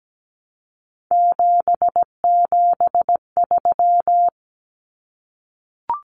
Morse code, heard as a keyed tone, about 17 wpm, 700 Hz. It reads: 773
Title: Common Cut Numbers - 3-Digit 17wpm